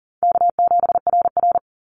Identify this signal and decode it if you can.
Morse code, heard as a keyed tone, about 40 wpm, 700 Hz. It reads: K7RR
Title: Easy Call Signs - 40wpm